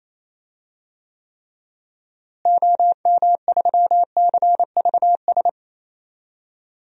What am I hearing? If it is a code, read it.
Morse code, heard as a keyed tone, about 27 wpm, 700 Hz. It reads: OM3CVS
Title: DXCC-100 Call Signs - 6 Characters - Repeated 3x - 28wpm